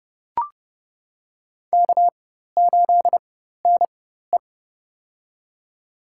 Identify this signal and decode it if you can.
Morse code, heard as a keyed tone, about 30 wpm, spaced as for 14 wpm, 700 Hz. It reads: K8NE
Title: USA Call Signs - 4x Character Spacing - 30wpm